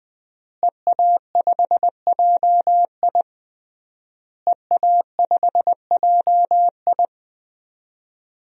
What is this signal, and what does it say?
Morse code, heard as a keyed tone, about 21 wpm, 700 Hz. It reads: EA5JI EA5JI
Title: DXCC-100 Call Signs - 5 Characters - Repeated 3x - 20wpm